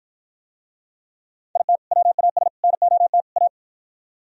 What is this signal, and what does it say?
Morse code, heard as a keyed tone, about 52 wpm, 700 Hz. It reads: IT WAS NOT A